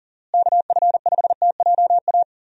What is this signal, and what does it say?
Morse code, heard as a keyed tone, about 40 wpm, 700 Hz. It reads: KF5TJA
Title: USA Call Signs - 6 Characters - Repeated 2x - 40wpm